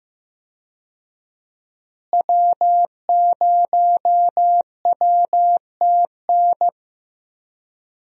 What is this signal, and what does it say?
Morse code, heard as a keyed tone, about 15 wpm, 700 Hz. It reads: W0WTN
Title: USA Call Signs - 5 Characters - Repeated 3x - 15wpm